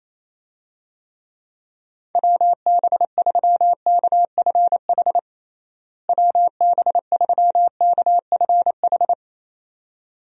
Morse code, heard as a keyed tone, about 28 wpm, 700 Hz. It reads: WB3KFH WB3KFH
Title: USA Call Signs - 6 Characters - Repeated 2x - 28wpm